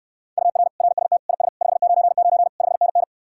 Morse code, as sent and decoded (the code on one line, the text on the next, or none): ..- .-. / .-. ... - / .. ... / ..... ----. ----. / ..... -. -.
UR RST IS 599 5NN